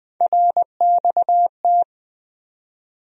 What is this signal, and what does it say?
Morse code, heard as a keyed tone, about 21 wpm, 700 Hz. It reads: RXT
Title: USA Call Sign Postfixes - 3 Letters 20wpm